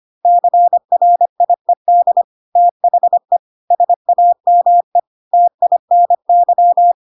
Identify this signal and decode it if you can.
Morse code, heard as a keyed tone, about 25 wpm, 700 Hz. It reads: CRIED THE SAME TINY
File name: Pinocchio / Pinocchio ch2 25 25 ik1hge